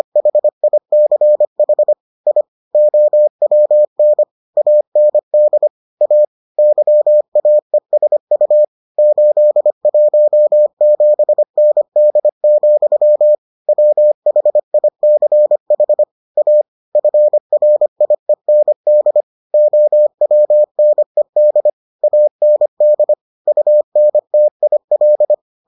Morse code, heard as a keyed tone, about 25 wpm, 600 Hz. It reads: HICH I OWN AND A YAESU 817ND, WHICH A FRIEND OWNED AND UNTIL